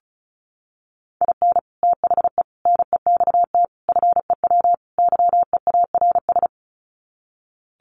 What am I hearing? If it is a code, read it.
Morse code, heard as a keyed tone, about 34 wpm, 700 Hz. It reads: IN THE NEXT FEW YEARS
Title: Sets of 5 Words Encore 35wpm